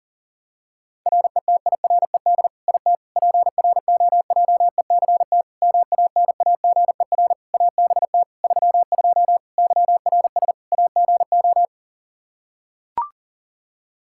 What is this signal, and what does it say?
Morse code, heard as a keyed tone, about 37 wpm, 700 Hz. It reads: RETIRED IT PROJECT MANAGER ABT 32 YRS AGO
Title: Practice QSO - Exchange 3 - More Info - 40wpm